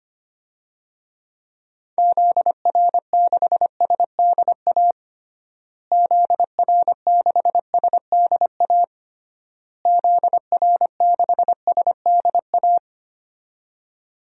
Morse code, heard as a keyed tone, about 25 wpm, 700 Hz. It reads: ZR6SDA ZR6SDA ZR6SDA
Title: DXCC-100 Call Signs - 6 Characters - Repeated 3x - 25wpm